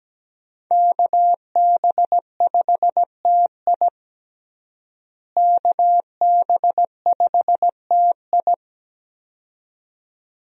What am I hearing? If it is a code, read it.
Morse code, heard as a keyed tone, about 17 wpm, 700 Hz. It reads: KB5TI KB5TI
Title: USA Call Signs - 5 Characters - Repeated 2x - 17wpm